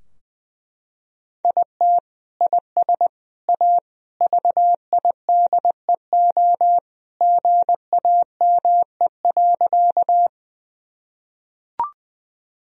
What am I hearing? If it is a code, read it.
Morse code, heard as a keyed tone, about 21 wpm, 700 Hz. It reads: IT IS A VIDEO GAME.
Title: Sentences from Top 1000 Words - Review Part 4 - 20wpm